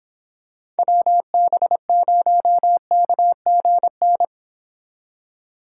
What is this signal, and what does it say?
Morse code, heard as a keyed tone, about 26 wpm, 700 Hz. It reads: WB0KGN